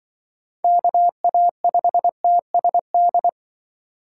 Morse code, heard as a keyed tone, about 24 wpm, 700 Hz. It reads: KA5TSD
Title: Speed Racing - USA 2x3 Call Signs 20wpm